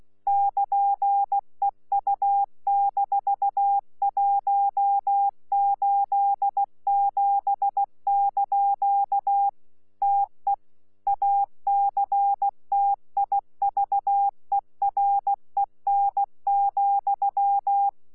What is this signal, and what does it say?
Morse code, heard as a keyed tone, about 16 wpm, 800 Hz. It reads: <KN>EU-187) TE ACTIVEREN,
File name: Z PI4VRZ / PI4VRZ 14-jun-2025